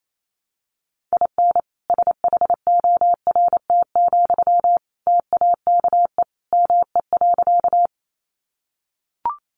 Morse code, heard as a keyed tone, about 28 wpm, 700 Hz. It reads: IN SHORT, TAKE ME.
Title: Sentences from Top 400 Words 28wpm